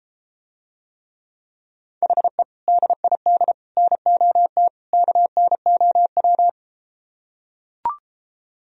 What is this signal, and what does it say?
Morse code, heard as a keyed tone, about 33 wpm, 700 Hz. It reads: HE DID NOT KNOW